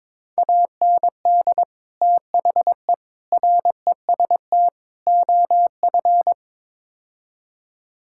Morse code, heard as a keyed tone, about 22 wpm, 700 Hz. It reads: AND THE REST OF
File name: practice / 22wpm / Build A Phrase - 5-Word Phrases - 22wpm